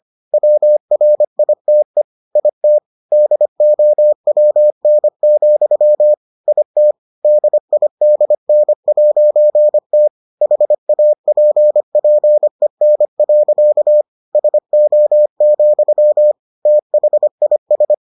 Morse code, heard as a keyed tone, about 25 wpm, 600 Hz. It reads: WRITE IT DOWN, IT DIDN'T HAPPEN. SO, THIS